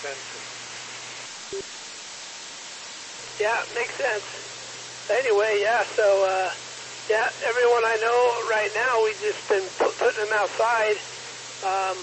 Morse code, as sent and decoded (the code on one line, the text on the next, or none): none